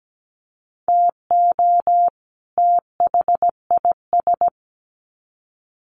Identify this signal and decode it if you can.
Morse code, heard as a keyed tone, about 17 wpm, 700 Hz. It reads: TO THIS
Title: Sets of 2 Words 17wpm